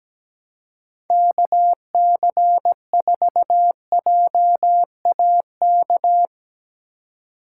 Morse code, heard as a keyed tone, about 17 wpm, 700 Hz. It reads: KC4JAK
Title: USA Call Signs - 6 Characters - 3x Character Spacing - 17wpm